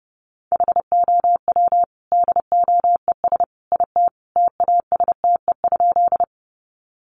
Morse code, heard as a keyed tone, about 30 wpm, 700 Hz. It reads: HOW DOES IT TASTE?
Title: Sentences from Top 1100 Words 30wpm